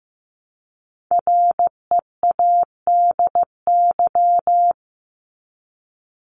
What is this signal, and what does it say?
Morse code, heard as a keyed tone, about 15 wpm, 700 Hz. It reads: READY
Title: Top 500 Words 15wpm